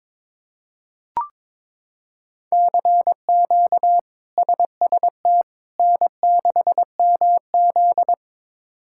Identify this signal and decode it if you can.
Morse code, heard as a keyed tone, about 22 wpm, 700 Hz. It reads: CQ SST N6MZ
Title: K1USN SST QSO Practice - 22wpm